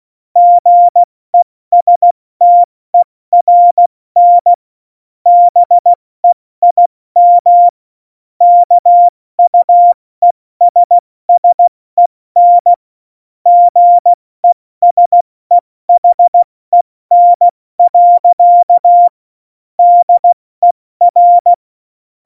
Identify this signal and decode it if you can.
Morse code, heard as a keyed tone, about 16 wpm, 700 Hz. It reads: GESTERN BEIM KUESSEN GESEHEN. DER
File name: Morsekurs / Witze / Witz 08. September 2020, 70 F80